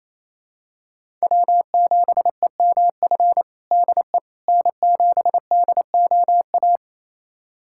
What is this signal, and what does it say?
Morse code, heard as a keyed tone, about 28 wpm, 700 Hz. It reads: W7EMF DE N7DOA
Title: USA Callsign DE Callsign - 5 Characters - 28wpm